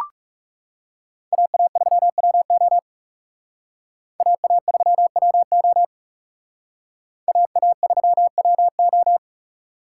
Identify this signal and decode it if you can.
Morse code, heard as a keyed tone, about 37 wpm, 700 Hz. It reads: AA3WO AA3WO AA3WO
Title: Speed Racing - USA 2x2 Call Signs 30wpm